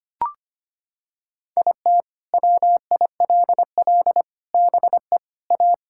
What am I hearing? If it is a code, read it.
Morse code, heard as a keyed tone, about 25 wpm, 700 Hz. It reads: IT WILL BE A
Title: Sentences from Top 1600 Words 25wpm